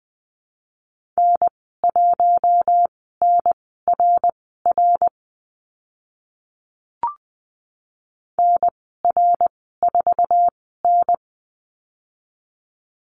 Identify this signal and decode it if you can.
Morse code, heard as a keyed tone, about 20 wpm, spaced as for 14 wpm, 700 Hz. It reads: N1NRR NR4N
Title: 069 Lesson 12g - Letter d - Call Signs 20wpm